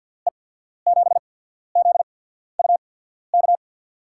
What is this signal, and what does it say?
Morse code, heard as a keyed tone, about 47 wpm, spaced as for 13 wpm, 700 Hz. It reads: E7ZUK
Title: USA Call Signs - 6 Characters - 8x Character Spacing - 50wpm